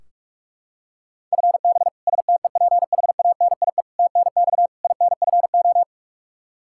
Morse code, heard as a keyed tone, about 46 wpm, 700 Hz. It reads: FB STEPHANIE TNX INFO
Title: Practice QSO - Exchange 3 - More Info - 45wpm